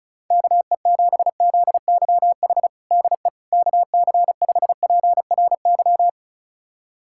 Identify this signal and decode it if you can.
Morse code, heard as a keyed tone, about 34 wpm, 700 Hz. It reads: KE7ZYH DE KC5PRY
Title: USA Callsign DE Callsign - 6 Characters - 35wpm